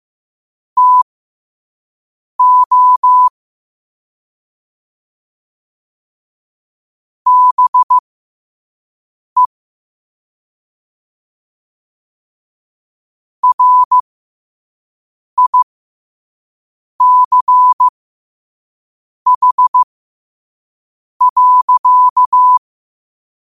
Morse code, heard as a keyed tone, about 15 wpm, spaced as for 5 wpm, 1000 Hz. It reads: TO BE RICH.
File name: qotdfiles / 5wpm